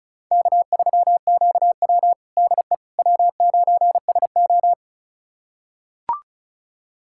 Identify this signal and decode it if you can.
Morse code, heard as a keyed tone, about 36 wpm, 700 Hz. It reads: K3QW DE W9SO
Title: USA Callsign DE Callsign - 4 Characters - 35wpm